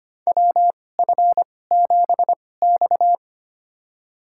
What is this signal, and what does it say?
Morse code, heard as a keyed tone, about 25 wpm, spaced as for 18 wpm, 700 Hz. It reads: WF7X
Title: USA Call Signs - 4 Characters - 2x Character Spacing - 25wpm